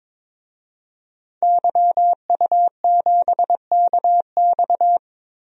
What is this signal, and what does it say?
Morse code, heard as a keyed tone, about 22 wpm, 700 Hz. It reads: YU7KX
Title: DXCC-100 Call Signs - 5 Characters - 22wpm